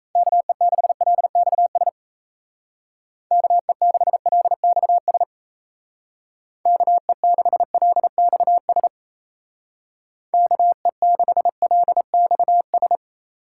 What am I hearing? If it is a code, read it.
Morse code, heard as a keyed tone, about 36 wpm, 700 Hz. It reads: KE6LXS KE6LXS KE6LXS KE6LXS